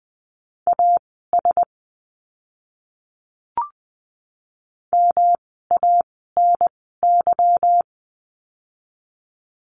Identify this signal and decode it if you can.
Morse code, heard as a keyed tone, about 20 wpm, spaced as for 14 wpm, 700 Hz. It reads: AS MANY
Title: 230 Lesson 37f - Letter x - Words 20wpm